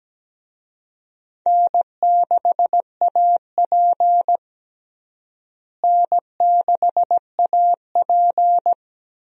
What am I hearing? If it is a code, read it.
Morse code, heard as a keyed tone, about 17 wpm, 700 Hz. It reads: N6AP N6AP
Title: USA Call Signs - 4 Characters - Repeated 2x - 17wpm